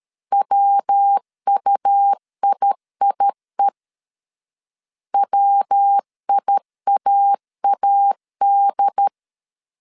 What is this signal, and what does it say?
Morse code, heard as a keyed tone, about 13 wpm, 800 Hz. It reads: WUIIE WIAAD
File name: CW / DET / 1 3 1500C S B 12wpm 2828